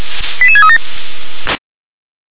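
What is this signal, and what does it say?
Five-tone selective call (ZVEI), heard as quick runs of five tones, tones of about 70 ms. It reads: 88637
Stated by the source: ZVEI five-tone selective call received by SDR.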